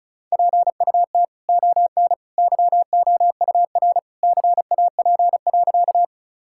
Morse code, heard as a keyed tone, about 34 wpm, 700 Hz. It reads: PUT ON YOUR CAP.
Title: Sentences from Top 1600 Words 35wpm